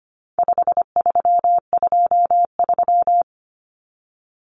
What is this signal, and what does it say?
Morse code, heard as a keyed tone, about 25 wpm, 700 Hz. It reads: 5323
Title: Numbers - 4-Digit 25wpm